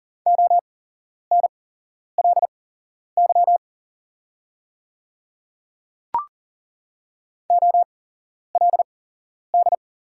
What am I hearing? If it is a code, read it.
Morse code, heard as a keyed tone, about 38 wpm, spaced as for 11 wpm, 700 Hz. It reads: ONLY OLD
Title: 8x Character Spacing - Top 500 Words - 40wpm